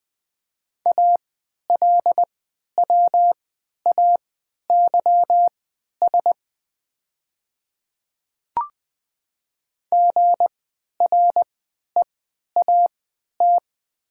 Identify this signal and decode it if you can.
Morse code, heard as a keyed tone, about 20 wpm, spaced as for 11 wpm, 700 Hz. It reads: ALWAYS GREAT
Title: 3x Character Spacing - Top 500 Words - 20wpm